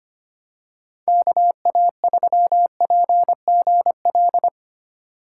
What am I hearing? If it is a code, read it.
Morse code, heard as a keyed tone, about 25 wpm, 700 Hz. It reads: KA3PGL
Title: USA Call Signs - 6 Characters - 8x Character Spacing - 25wpm